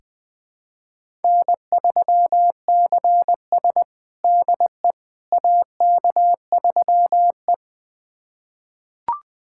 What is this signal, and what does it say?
Morse code, heard as a keyed tone, about 20 wpm, 700 Hz. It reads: N3CS DE AK3E
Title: USA Callsign DE Callsign - 4 Characters - 20wpm